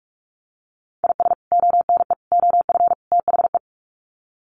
Morse code, heard as a keyed tone, about 44 wpm, 700 Hz. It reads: IS ONE OF THE
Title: Sets of 4 Words 45wpm